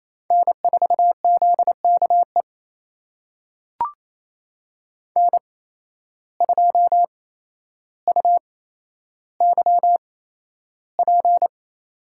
Morse code, heard as a keyed tone, about 28 wpm, spaced as for 12 wpm, 700 Hz. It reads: N4ZKE N2UYP
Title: USA Call Signs - 5 Characters - 8x Character Spacing - 28wpm